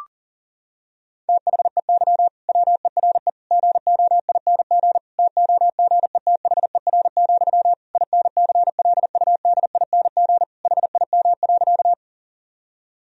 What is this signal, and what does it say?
Morse code, heard as a keyed tone, about 40 wpm, 700 Hz. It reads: THEY WERE GOING TOGETHER, INCLUDING HIM.
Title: Sentences from Top 1000 Words - Review Part 4 - 40wpm